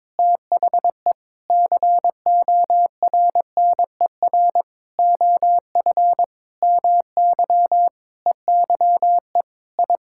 Morse code, heard as a keyed tone, about 21 wpm, 700 Hz. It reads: THE CORNER OF MY EYE I
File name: practice / 22wpm / Sets of 8 Words - 22wpm